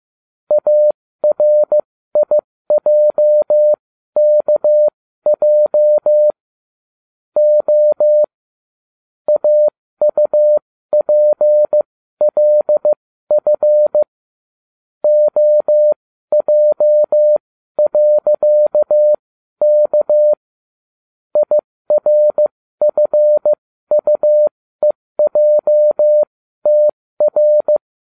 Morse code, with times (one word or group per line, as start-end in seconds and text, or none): 0.5-6.3 ARIJKJ
7.4-8.2 O
9.3-14.0 AUPLF
15.0-20.3 OJ.K
21.4-27.8 IRFUEJTR